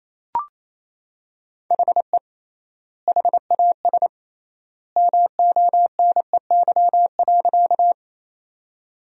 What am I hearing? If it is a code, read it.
Morse code, heard as a keyed tone, about 28 wpm, 700 Hz. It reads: HE HAS MONEY.
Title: 3x Word-Spacing - Sentences from Top 400 Words - 28wpm